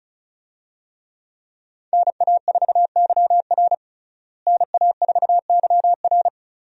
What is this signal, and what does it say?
Morse code, heard as a keyed tone, about 36 wpm, 700 Hz. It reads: NA4YR NA4YR
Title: USA Call Signs - 5 Characters - Repeated 3x - 35wpm